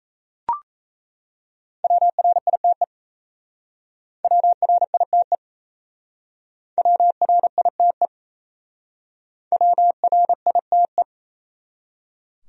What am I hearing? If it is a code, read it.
Morse code, heard as a keyed tone, about 37 wpm, 700 Hz. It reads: WRITE WRITE WRITE WRITE